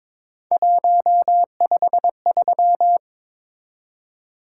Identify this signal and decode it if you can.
Morse code, heard as a keyed tone, about 22 wpm, 700 Hz. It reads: 153
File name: practice / 22wpm / Three Letter-Number Combinations - 22wpm